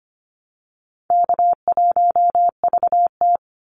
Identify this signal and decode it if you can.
Morse code, heard as a keyed tone, about 25 wpm, 700 Hz. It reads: K1VT
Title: Call Signs - 25wpm